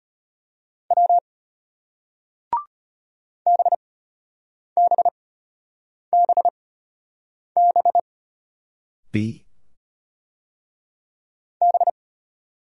Morse code, heard as a keyed tone, about 33 wpm, spaced as for 8 wpm, 700 Hz. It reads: W BBBB B